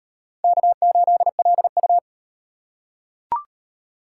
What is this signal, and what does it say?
Morse code, heard as a keyed tone, about 36 wpm, 700 Hz. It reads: K8LU